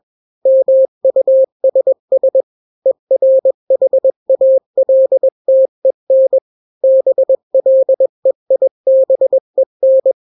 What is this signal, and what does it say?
Morse code, heard as a keyed tone, about 21 wpm, 530 Hz. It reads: MUSS ERHALTEN BLEIBEN
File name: daily / 20wpm